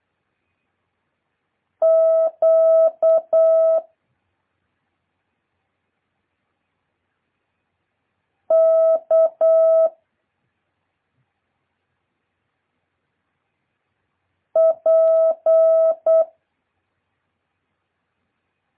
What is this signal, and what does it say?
Morse code, heard as a keyed tone, about 8 wpm, 650 Hz. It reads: Q K P
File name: jh8cbh / 2023cw